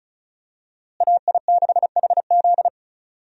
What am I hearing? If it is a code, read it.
Morse code, heard as a keyed tone, about 34 wpm, 700 Hz. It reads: AI6HZ
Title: USA Call Signs - 5 Characters - Repeated 3x - 35wpm